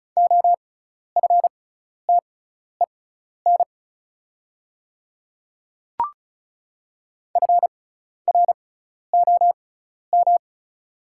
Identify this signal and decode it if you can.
Morse code, heard as a keyed tone, about 36 wpm, spaced as for 12 wpm, 700 Hz. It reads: OFTEN FROM
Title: 6x Character Spacing - Top 500 Words - 35wpm